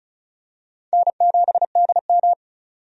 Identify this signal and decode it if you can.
Morse code, heard as a keyed tone, about 36 wpm, 700 Hz. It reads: N7DM
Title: USA Call Signs - 4 Characters - 35wpm